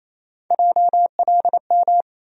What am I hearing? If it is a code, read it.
Morse code, heard as a keyed tone, about 28 wpm, 700 Hz. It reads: JLM
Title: USA Call Sign Postfixes - 3 Letters 28wpm